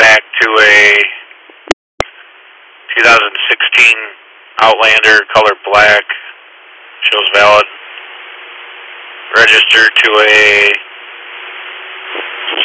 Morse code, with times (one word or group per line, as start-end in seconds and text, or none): none